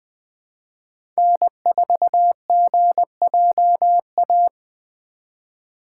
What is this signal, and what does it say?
Morse code, heard as a keyed tone, about 20 wpm, 700 Hz. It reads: N4GJA
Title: USA Call Signs - 5 Characters - 4x Character Spacing - 20wpm